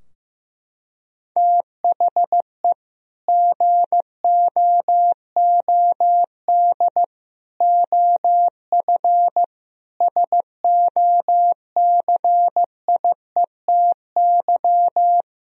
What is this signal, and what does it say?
Morse code, heard as a keyed tone, about 15 wpm, 700 Hz. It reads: THE GOOD OF SOCIETY
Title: Sentences from Top 1000 Words - Review Part 4 - 15wpm